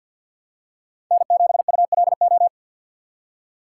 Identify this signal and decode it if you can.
Morse code, heard as a keyed tone, about 47 wpm, 700 Hz. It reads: N7ULO
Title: USA Call Signs - 5 Characters - 3x Character Spacing - 50wpm